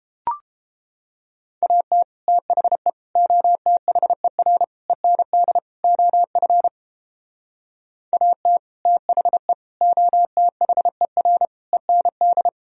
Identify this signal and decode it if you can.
Morse code, heard as a keyed tone, about 31 wpm, 700 Hz. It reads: AT THE OTHER END OF AT THE OTHER END